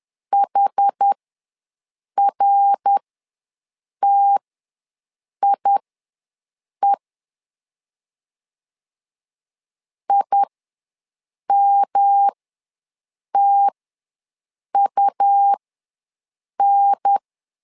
Morse code, heard as a keyed tone, about 10 wpm, spaced as for 6 wpm, 800 Hz. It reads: HRTIE IMTUN
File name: LEZIONI / 3 MWRO / 3 MWR ws 10wpm 2151